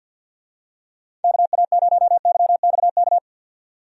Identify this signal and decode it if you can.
Morse code, heard as a keyed tone, about 53 wpm, 700 Hz. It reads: KA0YXK
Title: USA Call Signs - 6 Characters - 50wpm